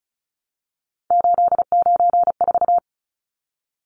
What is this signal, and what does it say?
Morse code, heard as a keyed tone, about 34 wpm, 700 Hz. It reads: 894
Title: Numbers - 3-Digit 35wpm